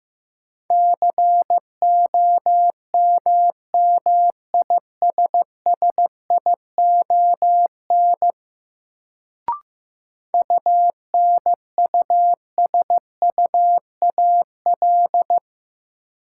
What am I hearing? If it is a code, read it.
Morse code, heard as a keyed tone, about 15 wpm, 700 Hz. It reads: COMMISSION UNUSUAL
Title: Sentences from Top 1400 Words - New Word Drill - 15wpm